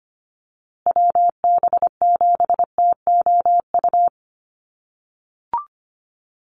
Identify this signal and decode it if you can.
Morse code, heard as a keyed tone, about 25 wpm, 700 Hz. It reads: WB7TOU
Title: USA Call Signs Repeated 2x 25wpm